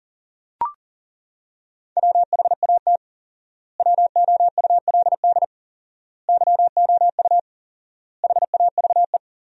Morse code, heard as a keyed tone, about 37 wpm, 700 Hz. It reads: WHAT WOULD YOU HAVE
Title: 4x Word-Spacing - Sentences from Top 300 Words - 40wpm